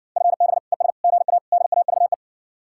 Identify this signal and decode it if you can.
Morse code, heard as a keyed tone, about 61 wpm, 700 Hz. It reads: FB ES GA DAVE